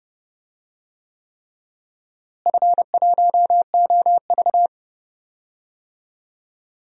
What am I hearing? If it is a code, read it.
Morse code, heard as a keyed tone, about 30 wpm, 700 Hz. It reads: F1OV
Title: DXCC-100 Call Signs - 4 Characters - Repeated 3x - 30wpm